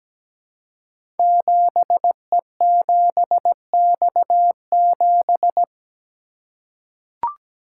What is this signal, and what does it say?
Morse code, heard as a keyed tone, about 17 wpm, 700 Hz. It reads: 7E7X7
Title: Code Groups - 5 Letter:Numbers - 17wpm